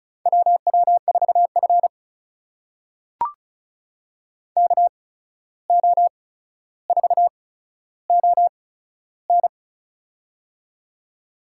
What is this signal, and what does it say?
Morse code, heard as a keyed tone, about 36 wpm, spaced as for 14 wpm, 700 Hz. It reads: WW4F KO4ON
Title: USA Call Signs - 8x Character Spacing - 35wpm